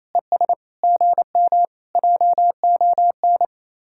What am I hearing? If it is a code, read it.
Morse code, heard as a keyed tone, about 26 wpm, 700 Hz. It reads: ES GM JON
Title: Speed-Racing Common QSO Phrases - 28wpm